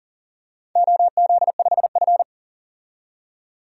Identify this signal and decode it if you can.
Morse code, heard as a keyed tone, about 40 wpm, 700 Hz. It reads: OZ5F
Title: DXCC-100 Call Signs - 4 Characters - 40wpm